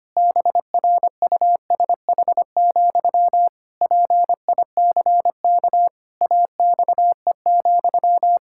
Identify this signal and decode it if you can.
Morse code, heard as a keyed tone, about 25 wpm, 700 Hz. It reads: BRUSH, PICK AXE,
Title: Pangram Challenge v2 - 25wpm